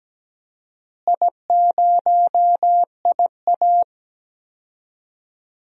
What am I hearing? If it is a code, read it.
Morse code, heard as a keyed tone, about 17 wpm, 700 Hz. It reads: I0IA
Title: DXCC-100 Call Signs - 4 Characters - Repeated 2x - 17wpm